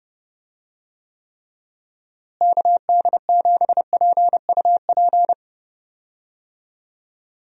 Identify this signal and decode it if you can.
Morse code, heard as a keyed tone, about 30 wpm, 700 Hz. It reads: KD7PUP
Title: USA Call Signs - 6 Characters - Repeated 2x - 30wpm